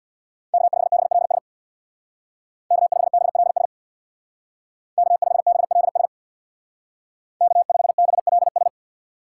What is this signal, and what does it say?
Morse code, heard as a keyed tone, about 64 wpm, 700 Hz. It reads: K5BLS K5BLS K5BLS K5BLS